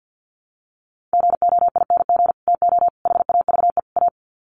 Morse code, heard as a keyed tone, about 50 wpm, 700 Hz. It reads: GOING TO HAVE A
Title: Sets of 4 Words 50wpm